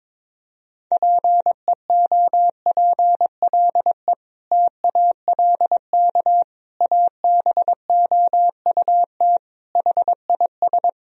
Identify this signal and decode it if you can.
Morse code, heard as a keyed tone, about 22 wpm, 700 Hz. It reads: PEOPLE TALK ABOUT HIS
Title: Sentences from Top 1000 Words - Review Part 3 - 22wpm